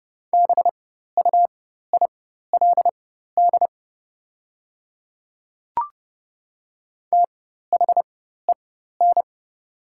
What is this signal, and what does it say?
Morse code, heard as a keyed tone, about 30 wpm, spaced as for 14 wpm, 690 Hz. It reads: BUILD THEN
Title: 4x Character Spacing - Top 500 Words - 30wpm